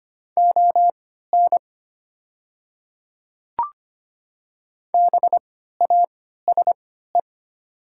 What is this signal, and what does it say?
Morse code, heard as a keyed tone, about 25 wpm, spaced as for 14 wpm, 700 Hz. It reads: ON BASE
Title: 3x Character Spacing - Top 500 Words - 25wpm